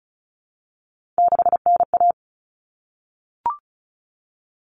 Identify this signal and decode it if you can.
Morse code, heard as a keyed tone, about 34 wpm, 700 Hz. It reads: 6NA
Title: Common Cut Numbers - 3-Digit 35wpm